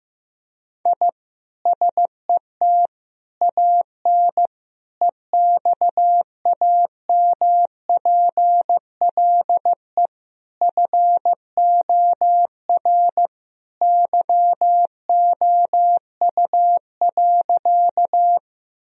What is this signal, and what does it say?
Morse code, heard as a keyed tone, about 15 wpm, 700 Hz. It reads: I SET AN EXAMPLE FOR YOU.
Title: Sentences from Top 1000 Words - Review Part 1 - 15wpm